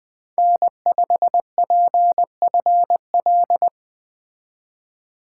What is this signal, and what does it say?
Morse code, heard as a keyed tone, about 20 wpm, 700 Hz. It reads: N5PFL
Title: Speed Racing - USA 1x3 Call Signs 17wpm